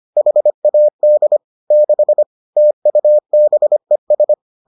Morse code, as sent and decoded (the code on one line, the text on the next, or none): .... .- -.. / -.... / - ..- -... ...
HAD 6 TUBES